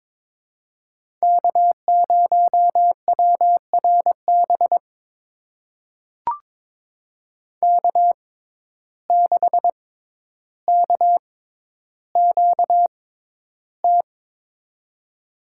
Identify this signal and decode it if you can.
Morse code, heard as a keyed tone, about 22 wpm, spaced as for 11 wpm, 700 Hz. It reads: K0WRB K6KQT
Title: USA Call Signs - 5 Characters - 6x Character Spacing - 22wpm